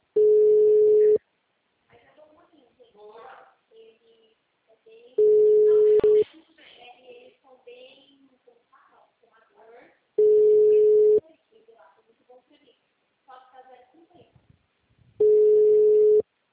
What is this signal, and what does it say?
Ringback tone (Europe, 425 Hz), with speech.
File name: out- -2002- - - .61015